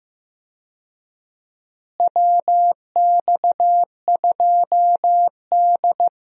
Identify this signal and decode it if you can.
Morse code, heard as a keyed tone, about 15 wpm, 700 Hz. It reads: WX2D
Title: USA Call Signs - 4 Characters - Repeated 2x - 15wpm